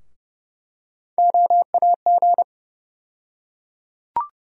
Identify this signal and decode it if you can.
Morse code, heard as a keyed tone, about 31 wpm, 700 Hz. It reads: OAG
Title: USA Call Sign Postfixes - 3 Letters 30wpm